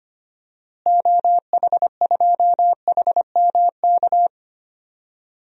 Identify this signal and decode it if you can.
Morse code, heard as a keyed tone, about 25 wpm, 700 Hz. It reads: OH2HMK
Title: DXCC-100 Call Signs - 6 Characters - Repeated 3x - 25wpm